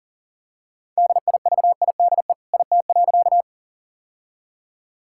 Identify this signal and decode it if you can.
Morse code, heard as a keyed tone, about 40 wpm, 700 Hz. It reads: DIVIDE IT.
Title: Sentences from Top 1600 Words 40wpm